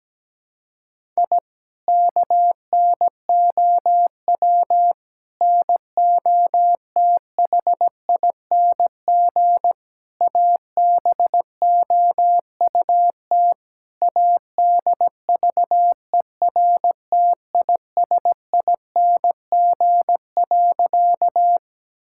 Morse code, heard as a keyed tone, about 17 wpm, 700 Hz. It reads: I KNOW NOTHING ABOUT ADVERTISING.
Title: Sentences from Top 1000 Words - Review Part 6 - 17wpm